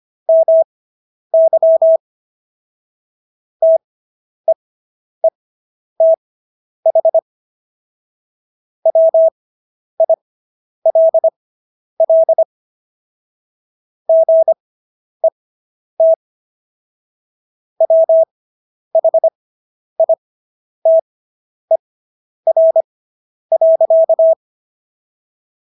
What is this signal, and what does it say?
Morse code, heard as a keyed tone, about 25 wpm, spaced as for 10 wpm, 650 Hz. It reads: MY TEETH WILL GET WHITER.